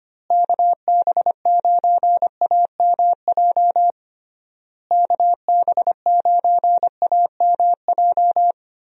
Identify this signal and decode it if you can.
Morse code, heard as a keyed tone, about 25 wpm, 700 Hz. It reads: KB9AMJ KB9AMJ
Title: USA Call Signs - 6 Characters - Repeated 3x - 25wpm